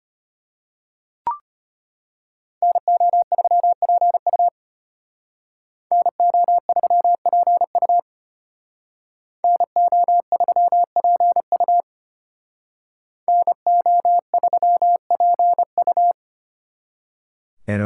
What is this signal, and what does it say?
Morse code, heard as a keyed tone, about 32 wpm, 700 Hz. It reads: NO3PU NO3PU NO3PU NO3PU